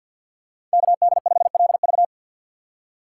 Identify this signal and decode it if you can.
Morse code, heard as a keyed tone, about 47 wpm, 700 Hz. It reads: KD5LV
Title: USA Call Signs - 5 Characters - 2x Character Spacing - 50wpm